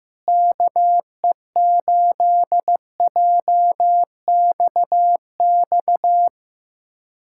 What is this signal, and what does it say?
Morse code, heard as a keyed tone, about 15 wpm, 700 Hz. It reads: KE8JXX
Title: USA Callsign DE Callsign - 6 Characters - 15wpm